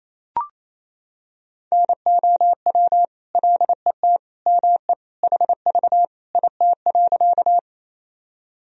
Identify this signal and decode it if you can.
Morse code, heard as a keyed tone, about 28 wpm, 700 Hz. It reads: NOW LET ME HV IT.
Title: Sentences from Top 300 Words and Abbreviations 28wpm